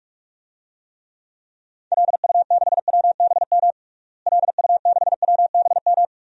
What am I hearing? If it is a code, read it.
Morse code, heard as a keyed tone, about 46 wpm, 700 Hz. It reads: LU6WBM LU6WBM
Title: DXCC-100 Call Signs - 6 Characters - Repeated 2x - 45wpm